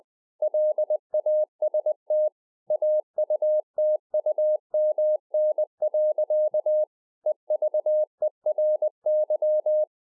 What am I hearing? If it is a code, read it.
Morse code, heard as a keyed tone, about 20 wpm, 610 Hz. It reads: LAST AUTUMN. EVERY